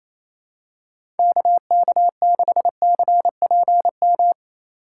Morse code, heard as a keyed tone, about 28 wpm, 700 Hz. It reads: KK6CPM
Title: USA Call Signs - 6 Characters - 4x Character Spacing - 28wpm